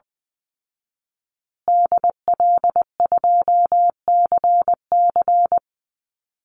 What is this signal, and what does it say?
Morse code, heard as a keyed tone, about 20 wpm, 700 Hz. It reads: DL2CC
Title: Call Signs Repeated 2x - 20wpm